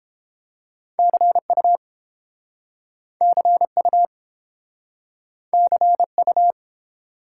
Morse code, heard as a keyed tone, about 32 wpm, 700 Hz. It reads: CU CU CU